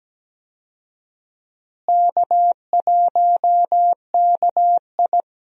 Morse code, heard as a keyed tone, about 17 wpm, 700 Hz. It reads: K1KI
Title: USA Call Signs - 4 Characters - Repeated 2x - 17wpm